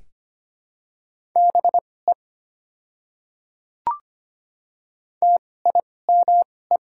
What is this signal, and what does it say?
Morse code, heard as a keyed tone, about 25 wpm, spaced as for 18 wpm, 690 Hz. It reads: BE TIME
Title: 2x Character Spacing - Top 100 Words - 25wpm